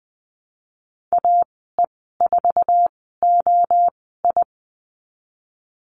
Morse code, heard as a keyed tone, about 20 wpm, spaced as for 15 wpm, 700 Hz. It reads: AE4OI
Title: 069 Lesson 12g - Letter d - Call Signs 20wpm